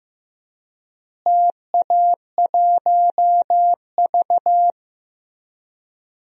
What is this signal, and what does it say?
Morse code, heard as a keyed tone, about 15 wpm, 700 Hz. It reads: TA1V
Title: DXCC-100 Call Signs - 4 Characters - Repeated 3x - 15wpm